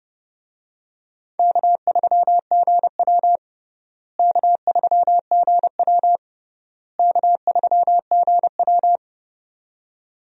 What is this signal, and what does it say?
Morse code, heard as a keyed tone, about 30 wpm, 700 Hz. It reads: K3GW K3GW K3GW
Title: USA Call Signs - 4 Characters - Repeated 3x - 30wpm